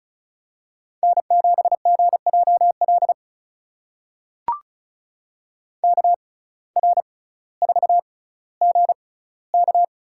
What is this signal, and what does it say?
Morse code, heard as a keyed tone, about 36 wpm, spaced as for 18 wpm, 700 Hz. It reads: N7GJL KR4GK
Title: USA Call Signs - 5 Characters - 6x Character Spacing - 35wpm